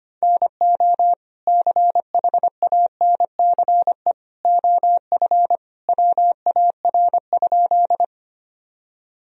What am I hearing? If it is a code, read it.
Morse code, heard as a keyed tone, about 25 wpm, 700 Hz. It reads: NO CHANCE OF WAR?
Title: Sentences from Top 1000 Words - Review Part 2 - 25wpm